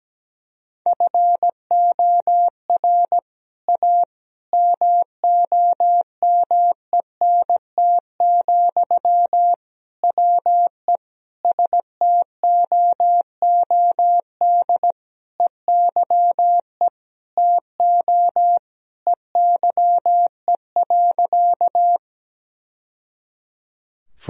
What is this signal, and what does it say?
Morse code, heard as a keyed tone, about 17 wpm, 700 Hz. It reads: FOR A MOMENT, WE STOOD EYE TO EYE.
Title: Sentences from Top 1000 Words - Review Part 2 - 17wpm